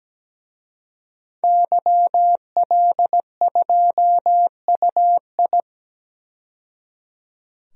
Morse code, heard as a keyed tone, about 17 wpm, 700 Hz. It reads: YL2UI
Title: DXCC-100 Call Signs - 5 Characters - 17wpm